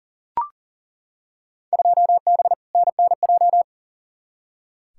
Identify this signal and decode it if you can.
Morse code, heard as a keyed tone, about 40 wpm, 700 Hz. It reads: 2B NNJ
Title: Field Day Exchanges Encore - 40wpm